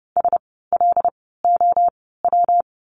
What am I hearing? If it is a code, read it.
Morse code, heard as a keyed tone, about 30 wpm, spaced as for 17 wpm, 700 Hz. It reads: SLOW
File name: course / 30:10wpm / 115 Lesson 19f - Letter w - Words 30wpm